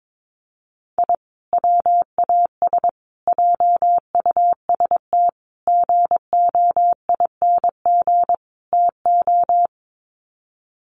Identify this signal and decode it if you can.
Morse code, heard as a keyed tone, about 22 wpm, 700 Hz. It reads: I WAS JUST GOING TO
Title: Sets of 5 Words Encore 22wpm